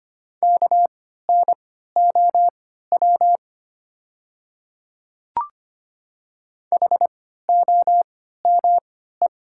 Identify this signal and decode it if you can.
Morse code, heard as a keyed tone, about 25 wpm, spaced as for 14 wpm, 700 Hz. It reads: KNOW HOME
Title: 3x Character Spacing - Top 200 Words - 25wpm